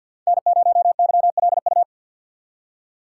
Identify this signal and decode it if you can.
Morse code, heard as a keyed tone, about 53 wpm, 700 Hz. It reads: N0YLU
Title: USA Call Signs - 5 Characters - 6x Character Spacing - 50wpm